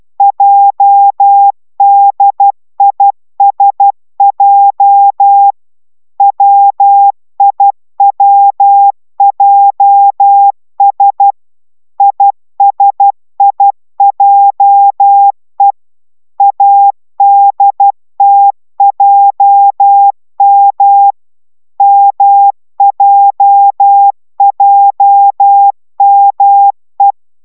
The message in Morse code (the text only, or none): JDISJ WIWJS ISIJE ADTJM MJJME